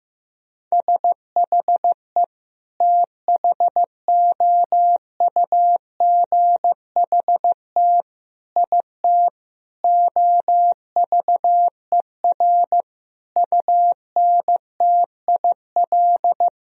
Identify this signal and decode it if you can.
Morse code, heard as a keyed tone, about 16 wpm, 700 Hz. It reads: SHE THOUGHT IT OVER UNTIL
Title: Sentences from Top 1000 Words - Review Part 1 - 15wpm